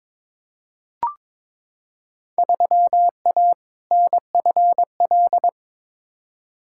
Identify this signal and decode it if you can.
Morse code, heard as a keyed tone, about 22 wpm, 700 Hz. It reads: 3A NFL
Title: Field Day Exchanges - 22wpm